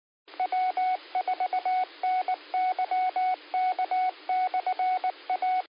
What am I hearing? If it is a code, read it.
Morse code, heard as a keyed tone, about 19 wpm, 740 Hz, with radio noise. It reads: W4NYK/A